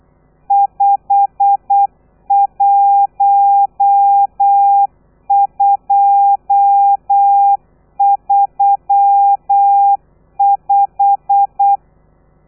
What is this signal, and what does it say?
Morse code, heard as a keyed tone, about 8 wpm, 800 Hz. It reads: 51235
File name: telegrafia / i2tfj / Lez12 35